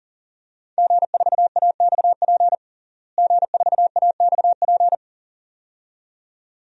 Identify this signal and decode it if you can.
Morse code, heard as a keyed tone, about 40 wpm, 700 Hz. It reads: G4AXP G4AXP
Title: DXCC-100 Call Signs - 5 Characters - Repeated 2x - 40wpm